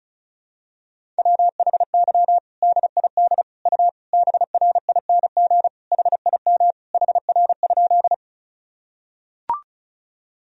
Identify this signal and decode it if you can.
Morse code, heard as a keyed tone, about 34 wpm, 700 Hz. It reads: WHY DID U BRING HIM HR?
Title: Sentences from Top 500 Words and Abbreviations 35wpm